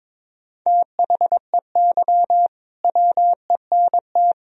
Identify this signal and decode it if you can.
Morse code, heard as a keyed tone, about 22 wpm, 700 Hz. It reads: THEY WENT
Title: Sentences from Top 1000 Words - Review Part 2 - 22wpm